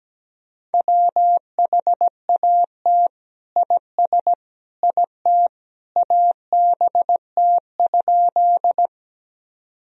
Morse code, heard as a keyed tone, about 17 wpm, 700 Hz. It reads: WHAT IS IT ABT?
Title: Sentences from Top 100 Words and Abbreviations 17wpm